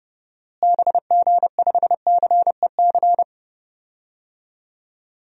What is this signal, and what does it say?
Morse code, heard as a keyed tone, about 30 wpm, 700 Hz. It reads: BG5CEC
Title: DXCC-100 Call Signs - 6 Characters - Repeated 3x - 30wpm